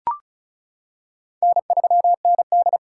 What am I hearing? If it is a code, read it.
Morse code, heard as a keyed tone, about 34 wpm, 700 Hz. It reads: N3ND
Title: K1USN SST QSO Practice - 35wpm